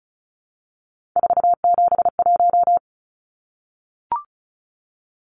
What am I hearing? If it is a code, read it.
Morse code, heard as a keyed tone, about 34 wpm, 700 Hz. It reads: 471
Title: Numbers - 3-Digit 35wpm